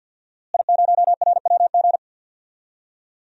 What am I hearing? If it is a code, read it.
Morse code, heard as a keyed tone, about 47 wpm, 700 Hz. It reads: I0RWG
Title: DXCC-100 Call Signs - 5 Characters - Repeated 2x - 50wpm